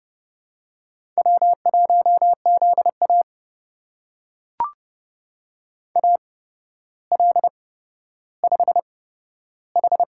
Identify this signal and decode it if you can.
Morse code, heard as a keyed tone, about 31 wpm, spaced as for 13 wpm, 700 Hz. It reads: W1ZA AL5H
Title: USA Call Signs - 4 Characters - 8x Character Spacing - 30wpm